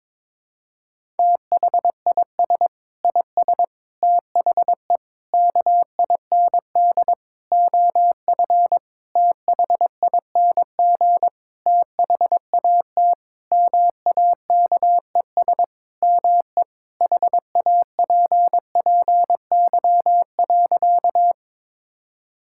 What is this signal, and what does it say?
Morse code, heard as a keyed tone, about 22 wpm, 700 Hz. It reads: THIS IS THE KIND OF THING THAT MAKES ME HAPPY.
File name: Sentences with 7-Word Phrases - 22wpm